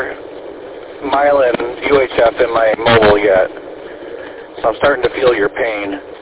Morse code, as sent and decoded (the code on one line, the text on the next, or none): none